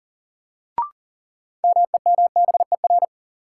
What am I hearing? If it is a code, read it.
Morse code, heard as a keyed tone, about 40 wpm, 700 Hz. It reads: MEMBER
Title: Sentences from Top 1100 Words - New Word Drill - 40wpm